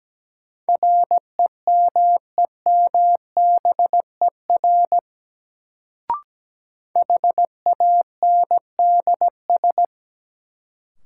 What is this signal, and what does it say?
Morse code, heard as a keyed tone, about 17 wpm, 700 Hz. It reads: REMEMBER HANDS
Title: Sets of 4 Words - New Word Drill - 17wpm